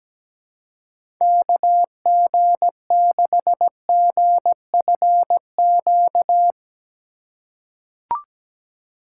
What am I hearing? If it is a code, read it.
Morse code, heard as a keyed tone, about 17 wpm, 700 Hz. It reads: KG6GFQ
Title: USA Call Signs - 6 Characters - 4x Character Spacing - 17wpm